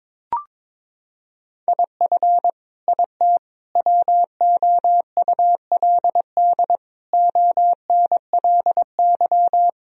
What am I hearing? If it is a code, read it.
Morse code, heard as a keyed tone, about 22 wpm, 700 Hz. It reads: IF IT WOULD ONLY
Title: Sentences from Top 1000 Words - Review Part 3 - 22wpm